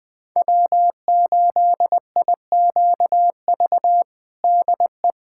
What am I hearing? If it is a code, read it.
Morse code, heard as a keyed tone, about 21 wpm, 700 Hz. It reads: W8IQV DE
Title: USA Callsign DE Callsign 20wpm